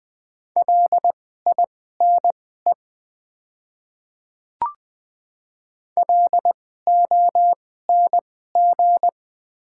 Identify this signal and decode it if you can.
Morse code, heard as a keyed tone, about 20 wpm, spaced as for 14 wpm, 700 Hz. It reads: LINE LONG
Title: 2x Character Spacing - Top 300 Words - 20wpm